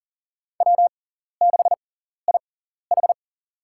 Morse code, heard as a keyed tone, about 40 wpm, spaced as for 14 wpm, 700 Hz. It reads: W6IH
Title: USA Call Signs - 4 Characters - 6x Character Spacing - 40wpm